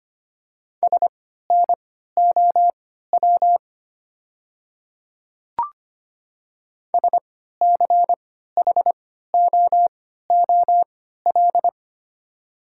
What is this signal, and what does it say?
Morse code, heard as a keyed tone, about 25 wpm, spaced as for 14 wpm, 690 Hz. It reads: SNOW SCHOOL
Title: 3x Character Spacing - Top 500 Words - 25wpm